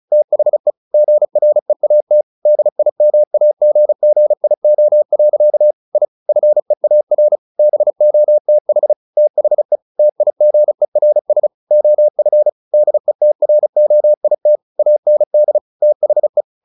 Morse code, heard as a keyed tone, about 34 wpm, 600 Hz. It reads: THE GREAT DIMAGGIO. I FEAR BOTH THE TIGERS OF DETROIT AND THE
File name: the old man and the sea / the old man and the sea 1 35wpm